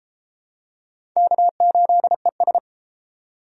Morse code, heard as a keyed tone, about 33 wpm, 700 Hz. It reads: K8ES